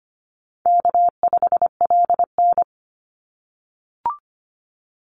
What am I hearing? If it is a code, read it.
Morse code, heard as a keyed tone, about 25 wpm, 700 Hz. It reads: K5LN
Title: Call Signs - 25wpm